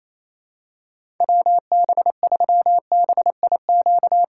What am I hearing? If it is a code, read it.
Morse code, heard as a keyed tone, about 28 wpm, 700 Hz. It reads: WB3BIQ
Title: K1USN SST QSO Practice - 28wpm